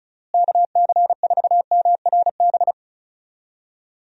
Morse code, heard as a keyed tone, about 36 wpm, 700 Hz. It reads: KC4MRB
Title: USA Call Signs - 6 Characters - Repeated 2x - 35wpm